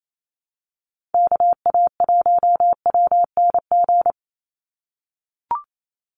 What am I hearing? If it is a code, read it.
Morse code, heard as a keyed tone, about 28 wpm, 700 Hz. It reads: KA1WNG
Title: USA Call Signs 28wpm